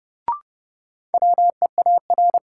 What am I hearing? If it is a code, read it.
Morse code, heard as a keyed tone, about 30 wpm, 700 Hz. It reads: WEAR
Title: Sentences from Top 1000 Words - Review Part 6 - New Word Drill - 30wpm